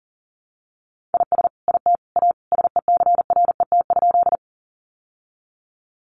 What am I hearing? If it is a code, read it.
Morse code, heard as a keyed tone, about 40 wpm, 700 Hz. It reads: IS IT A SECRET?
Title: Sentences from Top 1200 Words 40wpm